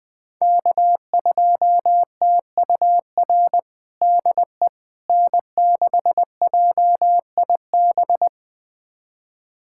Morse code, heard as a keyed tone, about 21 wpm, 700 Hz. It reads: K2TUR DE N6JIB
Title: USA Callsign DE Callsign - 5 Characters - 20wpm